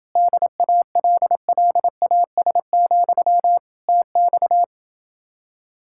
Morse code, heard as a keyed tone, about 27 wpm, 700 Hz. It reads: DALLAS, TX